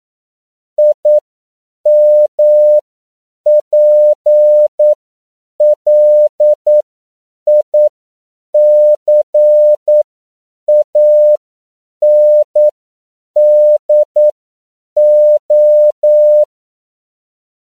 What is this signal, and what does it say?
Morse code, heard as a keyed tone, about 9 wpm, spaced as for 7 wpm, 600 Hz. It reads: IMPLICANDO
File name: CW / 10PPM